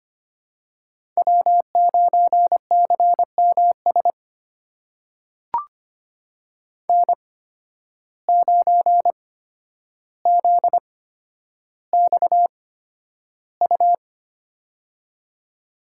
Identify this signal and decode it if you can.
Morse code, heard as a keyed tone, about 25 wpm, spaced as for 11 wpm, 700 Hz. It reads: W9CMS N9ZXU
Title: USA Call Signs - 5 Characters - 8x Character Spacing - 25wpm